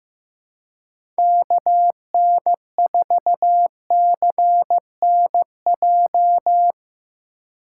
Morse code, heard as a keyed tone, about 15 wpm, 700 Hz. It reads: KN4CNJ
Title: USA Call Signs - 6 Characters - 3x Character Spacing - 15wpm